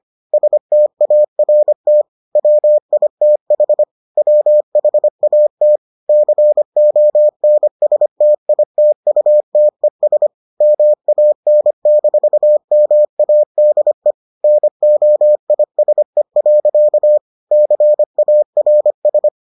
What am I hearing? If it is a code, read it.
Morse code, heard as a keyed tone, about 25 wpm, 600 Hz. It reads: START WITH WHAT CONSTITUTES MAN-MADE NOISE. CARS